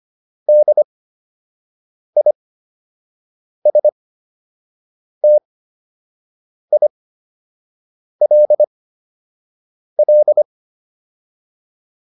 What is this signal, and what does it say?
Morse code, heard as a keyed tone, about 25 wpm, spaced as for 6 wpm, 600 Hz. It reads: DISTILL